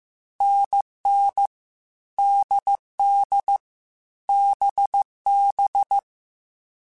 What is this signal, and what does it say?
Morse code, heard as a keyed tone, about 15 wpm, 800 Hz. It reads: NN DD BB
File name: Short Runs / Short Runs-mp3s / NDB NDB6 @13